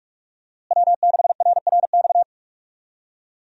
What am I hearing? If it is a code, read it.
Morse code, heard as a keyed tone, about 46 wpm, 700 Hz. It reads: W6RRX
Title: USA Call Signs - 5 Characters - Repeated 2x - 45wpm